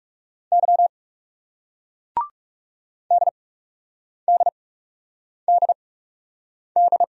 Morse code, heard as a keyed tone, about 38 wpm, spaced as for 8 wpm, 700 Hz. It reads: Y DDDD